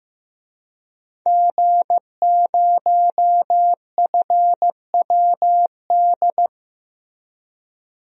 Morse code, heard as a keyed tone, about 15 wpm, 700 Hz. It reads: G0FWD
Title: DXCC-100 Call Signs - 5 Characters - Repeated 2x - 15wpm